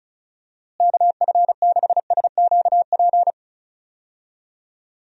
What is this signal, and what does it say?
Morse code, heard as a keyed tone, about 36 wpm, 700 Hz. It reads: KF6SQP
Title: USA Call Signs - 6 Characters - Repeated 2x - 35wpm